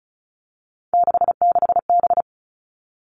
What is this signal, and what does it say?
Morse code, heard as a keyed tone, about 34 wpm, 700 Hz. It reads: 66B
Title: All Cut Numbers - 3-Digit 35wpm